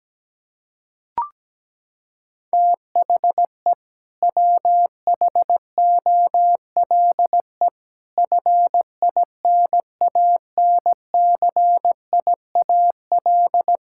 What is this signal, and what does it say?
Morse code, heard as a keyed tone, about 17 wpm, 700 Hz. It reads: THE WHOLE FINANCIAL
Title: Sentences from Top 1000 Words - Review Part 5 - 17wpm